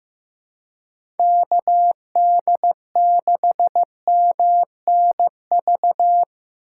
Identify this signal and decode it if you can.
Morse code, heard as a keyed tone, about 16 wpm, 700 Hz. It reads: KD6MNV
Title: USA Call Signs - 6 Characters - 2x Character Spacing - 15wpm